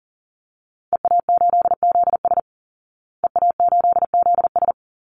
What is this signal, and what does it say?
Morse code, heard as a keyed tone, about 40 wpm, 700 Hz. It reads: EA8ZS EA8ZS
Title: Call Signs Repeated 2x - 40wpm